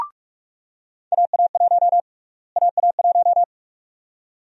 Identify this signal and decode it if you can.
Morse code, heard as a keyed tone, about 45 wpm, 700 Hz. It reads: AA1 AA1
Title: USA Call Sign Prefixes - Repeated 2x - 45wpm